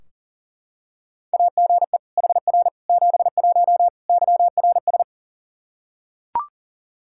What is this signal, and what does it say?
Morse code, heard as a keyed tone, about 40 wpm, 700 Hz. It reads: AGE HR 71 YRS
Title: Practice QSO - Exchange 3 - More Info - 40wpm